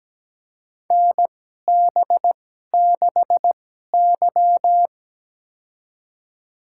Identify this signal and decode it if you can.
Morse code, heard as a keyed tone, about 17 wpm, spaced as for 12 wpm, 700 Hz. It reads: NB6Y
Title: USA Call Signs - 4 Characters - 2x Character Spacing - 17wpm